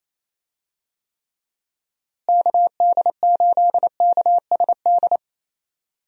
Morse code, heard as a keyed tone, about 28 wpm, 700 Hz. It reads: KD8KSD
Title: USA Call Signs - 6 Characters - Repeated 2x - 28wpm